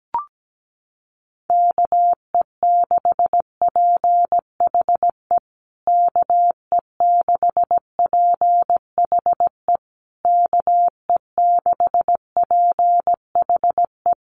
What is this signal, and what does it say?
Morse code, heard as a keyed tone, about 17 wpm, 700 Hz. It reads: KE6PHE KE6PHE KE6PHE
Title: USA Call Signs Repeated 3x 17wpm